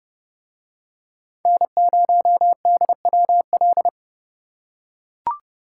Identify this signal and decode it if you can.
Morse code, heard as a keyed tone, about 29 wpm, 700 Hz. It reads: N0DWL